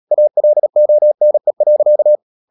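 Morse code, heard as a keyed tone, about 37 wpm, 600 Hz. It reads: ALONE.
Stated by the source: Morse code practice at 35–39 wpm sending English text of Ken Follett's novel Never, from CW med Gnister.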